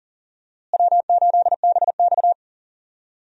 Morse code, heard as a keyed tone, about 40 wpm, 700 Hz. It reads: W8BX
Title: USA Call Signs - 4 Characters - Repeated 3x - 40wpm